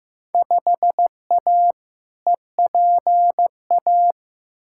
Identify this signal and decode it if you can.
Morse code, heard as a keyed tone, about 16 wpm, 700 Hz. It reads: 5A EPA
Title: Field Day Exchanges - 15wpm